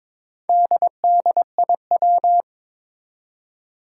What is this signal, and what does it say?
Morse code, heard as a keyed tone, about 22 wpm, 700 Hz. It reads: DDIW
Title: Four Challenging Letter-Number Combinations 22wpm